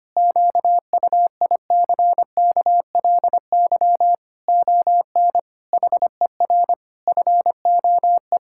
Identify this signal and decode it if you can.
Morse code, heard as a keyed tone, about 25 wpm, 700 Hz. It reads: QUICKLY ON HER FOE
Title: Pangram Challenge v2 - 25wpm